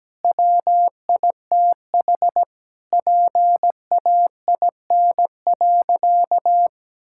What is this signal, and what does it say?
Morse code, heard as a keyed tone, about 17 wpm, 700 Hz. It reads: WITH PAIN.
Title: Sentences from Top 1600 Words 17wpm